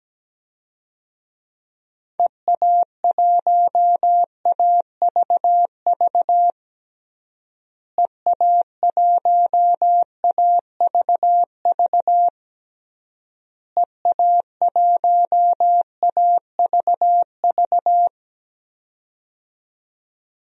Morse code, heard as a keyed tone, about 17 wpm, 700 Hz. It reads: EA1AVV EA1AVV EA1AVV
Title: DXCC-100 Call Signs - 6 Characters - Repeated 3x - 17wpm